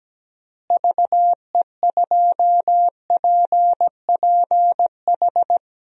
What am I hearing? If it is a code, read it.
Morse code, heard as a keyed tone, about 17 wpm, 700 Hz. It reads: VE2PPH
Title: DXCC-100 Call Signs - 6 Characters - Repeated 3x - 17wpm